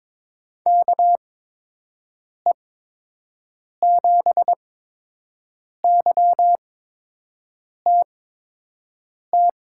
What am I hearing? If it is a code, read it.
Morse code, heard as a keyed tone, about 22 wpm, spaced as for 6 wpm, 700 Hz. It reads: KE7YTT
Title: USA Call Signs - 6 Characters - 8x Character Spacing - 22wpm